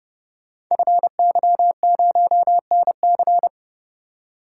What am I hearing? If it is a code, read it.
Morse code, heard as a keyed tone, about 30 wpm, 700 Hz. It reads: FY0NC
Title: Code Groups - 5 Letter:Numbers - 30wpm